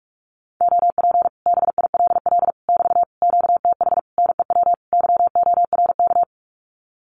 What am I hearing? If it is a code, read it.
Morse code, heard as a keyed tone, about 44 wpm, 700 Hz. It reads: OP BILL <BT> QTH NEW YORK
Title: Common QSO Phrases Encore 45wpm